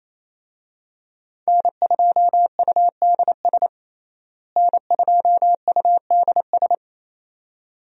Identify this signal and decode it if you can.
Morse code, heard as a keyed tone, about 28 wpm, 700 Hz. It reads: N2UDS N2UDS
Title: USA Call Signs - 5 Characters - Repeated 2x - 28wpm